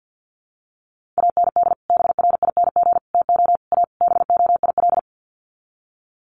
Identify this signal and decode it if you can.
Morse code, heard as a keyed tone, about 50 wpm, 700 Hz. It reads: AND BRING TO A BOIL
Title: Sets of 5 Words Encore 50wpm